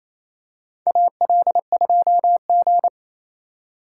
Morse code, heard as a keyed tone, about 28 wpm, 700 Hz. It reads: AL2G
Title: USA Call Signs - 4 Characters - 2x Character Spacing - 28wpm